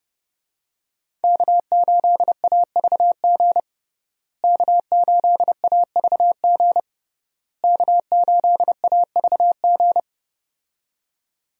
Morse code, heard as a keyed tone, about 30 wpm, 700 Hz. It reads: K8AVG K8AVG K8AVG
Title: USA Call Signs - 5 Characters - Repeated 3x - 30wpm